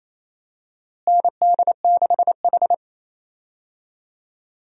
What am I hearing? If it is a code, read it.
Morse code, heard as a keyed tone, about 28 wpm, 700 Hz. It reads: ND6H
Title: USA Call Signs - 4 Characters - Repeated 2x - 28wpm